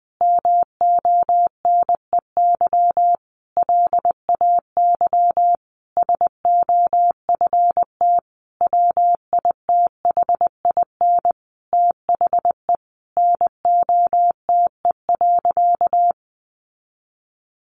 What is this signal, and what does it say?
Morse code, heard as a keyed tone, about 20 wpm, 700 Hz. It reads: MONEY LAY SOFT WITHIN THE NOTE.
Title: Sentences from Top 1500 Words 20wpm